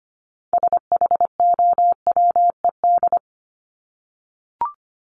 Morse code, heard as a keyed tone, about 25 wpm, 700 Hz. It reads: SHOWED
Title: ED Suffix - Top 100 Words with Base Words 25wpm